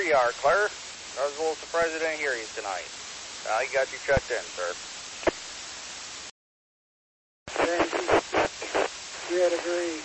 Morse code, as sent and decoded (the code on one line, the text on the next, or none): none